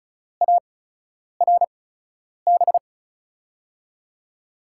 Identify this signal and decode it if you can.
Morse code, heard as a keyed tone, about 36 wpm, spaced as for 10 wpm, 700 Hz. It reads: ARB
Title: USA Call Signs - 6 Characters - 8x Character Spacing - 35wpm